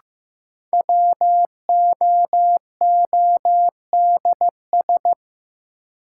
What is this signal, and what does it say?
Morse code, heard as a keyed tone, about 15 wpm, 700 Hz. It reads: WOODS
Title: Sentences from Top 1000 Words - Review Part 2 - New Word Drill - 15wpm